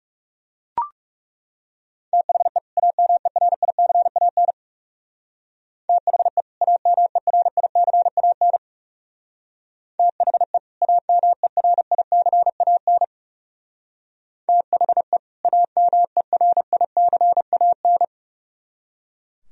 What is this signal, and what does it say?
Morse code, heard as a keyed tone, about 37 wpm, 700 Hz. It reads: THE AMERICAN THE AMERICAN THE AMERICAN THE AMERICAN
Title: Speed-Racing - Sets of 2 Words 30wpm